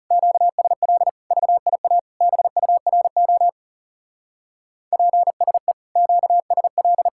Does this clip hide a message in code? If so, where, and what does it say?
Yes: Morse code from 0.1 to 7.1 s, heard as a keyed tone, about 38 wpm, 700 Hz. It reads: QSL VIA BURO PSE QSL